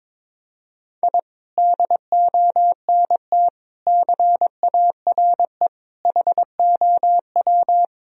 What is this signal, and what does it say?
Morse code, heard as a keyed tone, about 21 wpm, 700 Hz. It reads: I DONT CARE HOW
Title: Sentences from Top 1000 Words - Review Part 3 - 22wpm